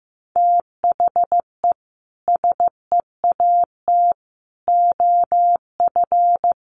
Morse code, heard as a keyed tone, about 15 wpm, 700 Hz. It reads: THE SEAT OF
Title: Sentences from Top 1500 Words 15wpm